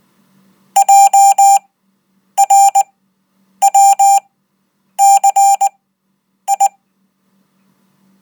Morse code, heard as a keyed tone, about 19 wpm, spaced as for 8 wpm, 790 Hz. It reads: JRWCI